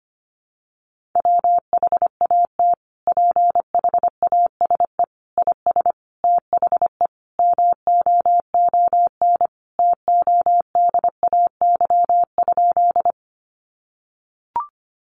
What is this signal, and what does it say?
Morse code, heard as a keyed tone, about 25 wpm, 700 Hz. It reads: WHAT PHASE IS THE MOON TODAY?
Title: Sentences from Top 1200 Words 25wpm